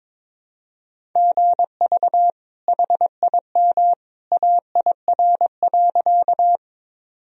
Morse code, heard as a keyed tone, about 22 wpm, 700 Hz. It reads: GV HIM AIR.
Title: Sentences from Top 200 Words and Abbreviations 22wpm